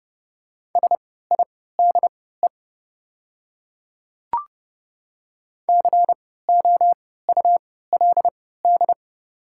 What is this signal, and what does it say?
Morse code, heard as a keyed tone, about 30 wpm, spaced as for 17 wpm, 700 Hz. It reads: SIDE COULD
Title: 3x Character Spacing - Top 100 Words - 30wpm